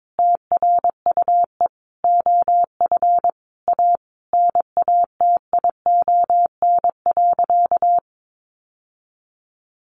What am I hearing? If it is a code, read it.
Morse code, heard as a keyed tone, about 22 wpm, 700 Hz. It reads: TRUE OF A NATION.
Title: Sentences from Top 1400 Words 22wpm